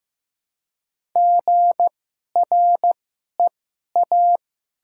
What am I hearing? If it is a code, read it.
Morse code, heard as a keyed tone, about 15 wpm, spaced as for 11 wpm, 700 Hz. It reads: GREA
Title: 2x Character Spacing - Top 200 Words - 15wpm